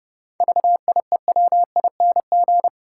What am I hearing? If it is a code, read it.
Morse code, heard as a keyed tone, about 30 wpm, 700 Hz. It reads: VIEWING
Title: Pangram Challenge v2 - 30wpm